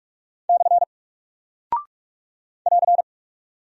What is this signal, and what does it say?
Morse code, heard as a keyed tone, about 44 wpm, 700 Hz. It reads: / <AR>
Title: Single Character (Letter:Number:Punctuation:Prosign) 45wpm